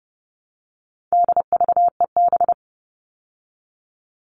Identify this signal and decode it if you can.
Morse code, heard as a keyed tone, about 30 wpm, 700 Hz. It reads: DVEB
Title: All Cut Numbers - 4-Digit 30wpm